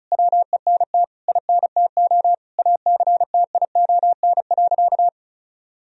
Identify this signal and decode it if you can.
Morse code, heard as a keyed tone, about 36 wpm, 700 Hz. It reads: WENT INTO ACTION.
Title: Sentences from Top 1000 Words - Review Part 4 - 35wpm